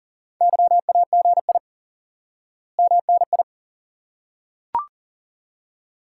Morse code, heard as a keyed tone, about 37 wpm, 700 Hz. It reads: YAGI MNI
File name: IQE Course 3 - Lesson 89 - YAGI - Two Elements - 40wpm